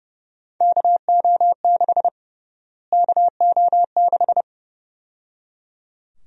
Morse code, heard as a keyed tone, about 30 wpm, 700 Hz. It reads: KO6 KO6
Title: USA Call Sign Prefixes - Repeated 2x - 30wpm